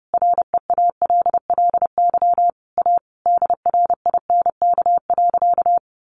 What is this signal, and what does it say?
Morse code, heard as a keyed tone, about 28 wpm, 700 Hz. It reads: REALLY A DRINK.
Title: Sentences from Top 1500 Words 30wpm